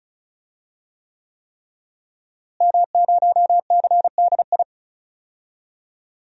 Morse code, heard as a keyed tone, about 34 wpm, 700 Hz. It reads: M0CDI
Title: DXCC-100 Call Signs - 5 Characters - Repeated 3x - 35wpm